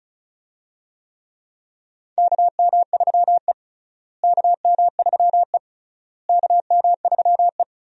Morse code, heard as a keyed tone, about 36 wpm, 700 Hz. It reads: KM3E KM3E KM3E
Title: USA Call Signs - 4 Characters - Repeated 3x - 35wpm